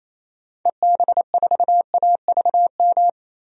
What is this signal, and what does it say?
Morse code, heard as a keyed tone, about 27 wpm, 700 Hz. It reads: EB4AVM
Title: DXCC-100 Call Signs - 6 Characters - Repeated 2x - 28wpm